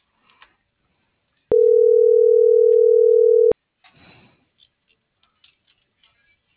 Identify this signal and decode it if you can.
Ringback tone (North America), 440 + 480 Hz.
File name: CENTRAL FRALSE / out fernanda.fralse